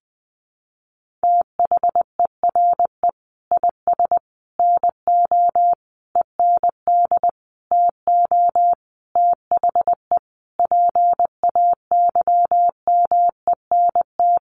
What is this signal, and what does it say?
Morse code, heard as a keyed tone, about 20 wpm, 700 Hz. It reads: THERE IS NO END TO THE PAYMENT
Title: Sentences from Top 1200 Words 20wpm